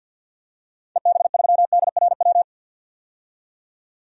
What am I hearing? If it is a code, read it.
Morse code, heard as a keyed tone, about 51 wpm, 700 Hz. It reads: EB3DRW
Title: DXCC-100 Call Signs - 6 Characters - 50wpm